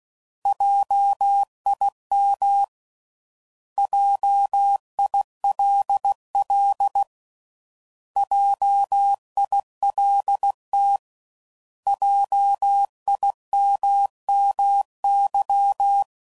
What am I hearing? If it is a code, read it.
Morse code, heard as a keyed tone, about 16 wpm, 800 Hz. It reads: JIM JILL JILT JIMMY